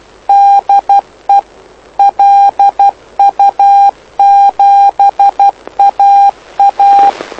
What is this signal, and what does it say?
Morse code, heard as a keyed tone, about 12 wpm, 800 Hz. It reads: DE LU7AA
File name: lu4aao / lu7aa 09dic2011 exp planeador vuelo1